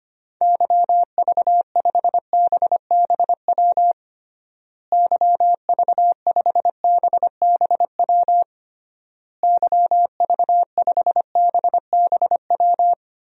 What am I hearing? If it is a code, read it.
Morse code, heard as a keyed tone, about 25 wpm, 700 Hz. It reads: YV5BBW YV5BBW YV5BBW
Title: DXCC-100 Call Signs - 6 Characters - Repeated 3x - 25wpm